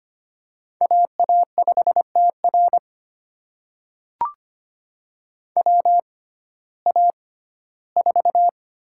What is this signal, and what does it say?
Morse code, heard as a keyed tone, about 25 wpm, spaced as for 15 wpm, 700 Hz. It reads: AA5TR WA4
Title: USA Call Signs - 6x Character Spacing - 25wpm